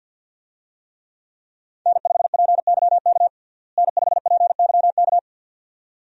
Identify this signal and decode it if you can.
Morse code, heard as a keyed tone, about 51 wpm, 700 Hz. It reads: N5PYK N5PYK
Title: USA Call Signs - 5 Characters - Repeated 2x - 50wpm